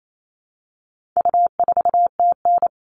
Morse code, heard as a keyed tone, about 28 wpm, 700 Hz. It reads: U4TN
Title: All Cut Numbers - 4-Digit 28wpm